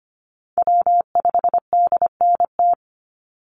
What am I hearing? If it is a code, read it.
Morse code, heard as a keyed tone, about 25 wpm, 700 Hz. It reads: W5DNT
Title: Call Signs Repeated 3x - 25wpm